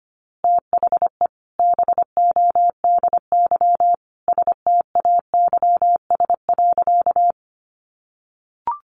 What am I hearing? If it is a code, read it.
Morse code, heard as a keyed tone, about 25 wpm, 700 Hz. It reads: THE BODY STAYS.
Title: A Sound Of Thunder 25wpm